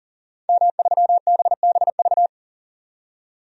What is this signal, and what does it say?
Morse code, heard as a keyed tone, about 40 wpm, 700 Hz. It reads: M3BBV
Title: DXCC-100 Call Signs - 5 Characters - 40wpm